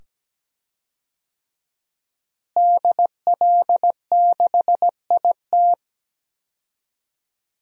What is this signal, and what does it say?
Morse code, heard as a keyed tone, about 17 wpm, 700 Hz. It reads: DL6IT
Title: DXCC-100 Call Signs - 5 Characters - Repeated 2x - 17wpm